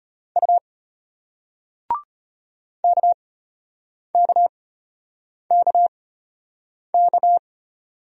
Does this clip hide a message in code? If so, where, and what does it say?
Yes: Morse code from 0.4 to 7.4 s, heard as a keyed tone, about 33 wpm, spaced as for 8 wpm, 700 Hz. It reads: U KKKK